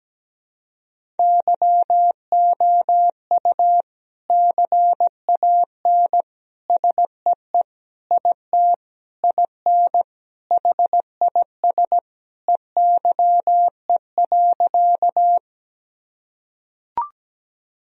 Morse code, heard as a keyed tone, about 17 wpm, 700 Hz. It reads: YOU CAN SEE IT IN HIS EYE.
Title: Sentences from Top 1000 Words - Review Part 1 - 17wpm